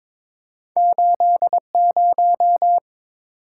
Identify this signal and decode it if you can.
Morse code, heard as a keyed tone, about 31 wpm, spaced as for 27 wpm, 700 Hz. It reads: 80
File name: Speed-Racing - Two Letter-Number Combinations - 22wpm